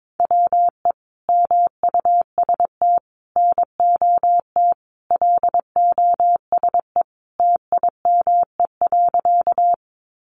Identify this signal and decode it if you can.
Morse code, heard as a keyed tone, about 22 wpm, 700 Hz. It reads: WE MUST NOT LOSE TIME.
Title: Sentences from Top 1100 Words 22wpm